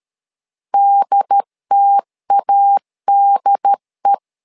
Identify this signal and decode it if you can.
Morse code, heard as a keyed tone, about 13 wpm, 800 Hz. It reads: DTADE
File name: LEZIONI / 2 UHDN / 2 UHD 12wpm 1540